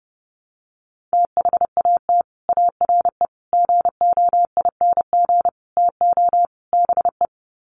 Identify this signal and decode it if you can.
Morse code, heard as a keyed tone, about 30 wpm, 700 Hz. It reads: THAT ARE GOING TO BE
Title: Sets of 5 Words Encore2 30wpm